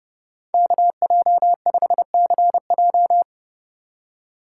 Morse code, heard as a keyed tone, about 27 wpm, 700 Hz. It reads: KJ5CJ